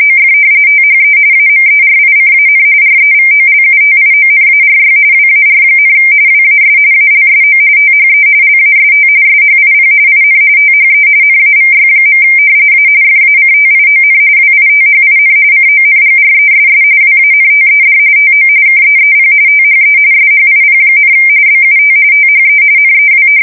RTTY, about 45.45 baud. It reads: OF THE AMERICAN REVOLUTIONARY WAR, WAS ALSO THE LAST SHIP IN THE NAVY. A FACTION WITHIN CONGRESS WANTED TO KEEP THE SHIP, BUT THE NEW
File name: fsk170 / usnavy1